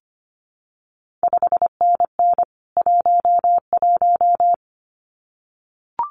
Morse code, heard as a keyed tone, about 25 wpm, 700 Hz. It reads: 5NN 11
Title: CQ WW DX Contest Exchange Encore - 25wpm